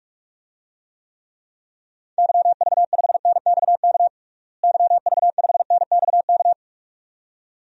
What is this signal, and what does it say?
Morse code, heard as a keyed tone, about 41 wpm, 700 Hz. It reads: YV5NXK YV5NXK
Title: DXCC-100 Call Signs - 6 Characters - Repeated 2x - 45wpm